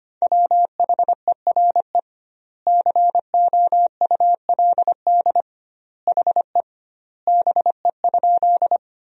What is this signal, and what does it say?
Morse code, heard as a keyed tone, about 25 wpm, 700 Hz. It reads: WHERE COULD HE BE?
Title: 2x Word-Spacing - Sentences from Top 200 Words - 25wpm